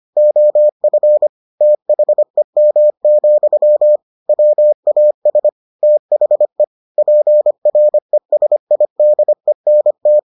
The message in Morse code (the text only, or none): OF THEM, WAS THE PRESIDENT